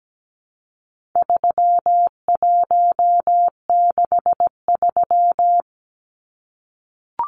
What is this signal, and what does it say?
Morse code, heard as a keyed tone, about 17 wpm, 700 Hz. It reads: 3163
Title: Numbers - 4-Digit 17wpm